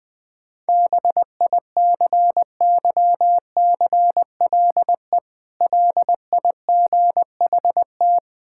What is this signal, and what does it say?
Morse code, heard as a keyed tone, about 20 wpm, 700 Hz. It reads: BICYCLE LIGHT
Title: Difficult Words - Speed-Racing with Hints - 20wpm